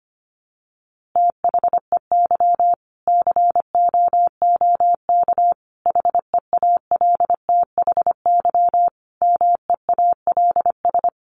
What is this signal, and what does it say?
Morse code, heard as a keyed tone, about 25 wpm, 700 Hz. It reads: THEY COOK HEALTHY MEALS
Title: Easy Phrases 25wpm